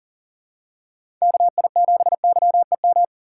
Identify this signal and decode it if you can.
Morse code, heard as a keyed tone, about 40 wpm, 700 Hz. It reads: KI7YEM
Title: USA Call Signs - 6 Characters - 4x Character Spacing - 40wpm